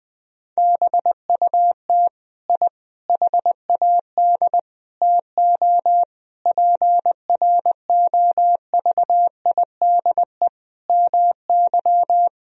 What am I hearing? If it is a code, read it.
Morse code, heard as a keyed tone, about 20 wpm, 700 Hz. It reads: BUT I HAD TO PROVIDE MY
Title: Sentences from Top 1000 Words - Review Part 3 - 20wpm